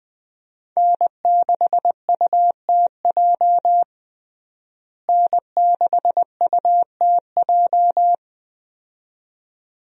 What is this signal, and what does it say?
Morse code, heard as a keyed tone, about 21 wpm, 700 Hz. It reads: N6UTJ N6UTJ
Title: USA Call Signs - 5 Characters - Repeated 2x - 20wpm